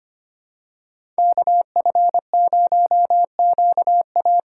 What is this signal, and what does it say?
Morse code, heard as a keyed tone, about 25 wpm, 700 Hz. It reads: KF0QA
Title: USA Call Signs - 5 Characters - 2x Character Spacing - 25wpm